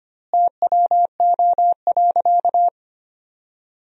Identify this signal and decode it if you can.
Morse code, heard as a keyed tone, about 25 wpm, 700 Hz. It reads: TWO.
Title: 2x Word-Spacing - Sentences from Top 200 Words - 25wpm